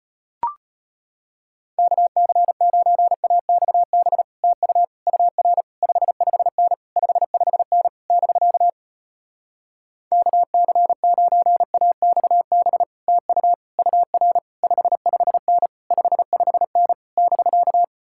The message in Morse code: KC9AXB TU UR 55N 55N <BK> KC9AXB TU UR 55N 55N <BK>

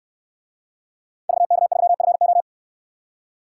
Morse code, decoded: VK3FO

